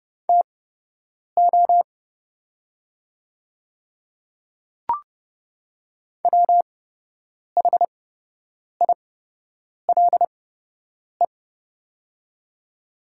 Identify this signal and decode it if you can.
Morse code, heard as a keyed tone, about 30 wpm, spaced as for 8 wpm, 700 Hz. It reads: TO WHILE
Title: 8x Character Spacing - Top 300 Words - 30wpm